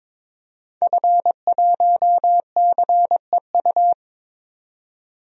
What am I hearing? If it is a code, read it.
Morse code, heard as a keyed tone, about 22 wpm, 700 Hz. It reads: F1CEU
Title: DXCC-100 Call Signs - 5 Characters - 22wpm